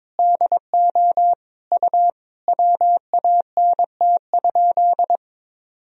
Morse code, heard as a keyed tone, about 22 wpm, 700 Hz. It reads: DO U WANT?
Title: Sentences from Top 200 Words and Abbreviations 22wpm